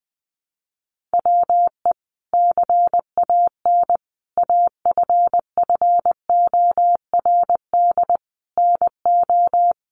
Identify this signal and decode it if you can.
Morse code, heard as a keyed tone, about 20 wpm, 700 Hz. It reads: WE CAN AFFORD NO
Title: Sentences from Top 1100 Words 20wpm